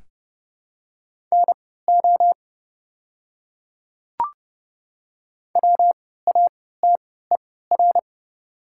Morse code, heard as a keyed tone, about 30 wpm, spaced as for 17 wpm, 700 Hz. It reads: NO WATER
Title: 3x Character Spacing - Top 100 Words - 30wpm